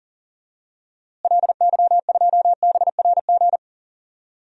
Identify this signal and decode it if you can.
Morse code, heard as a keyed tone, about 37 wpm, 700 Hz. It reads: LY2BRG
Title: DXCC-100 Call Signs - 6 Characters - Repeated 3x - 40wpm